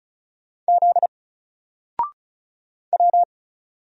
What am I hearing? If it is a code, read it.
Morse code, heard as a keyed tone, about 36 wpm, 700 Hz. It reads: Z W
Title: Single Letter-Number Alt 35wpm